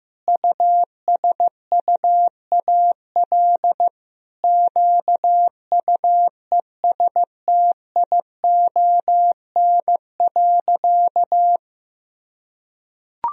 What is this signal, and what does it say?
Morse code, heard as a keyed tone, about 15 wpm, 700 Hz. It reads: USUAL QUESTION.
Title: Sentences from Top 1000 Words - Review Part 2 - 15wpm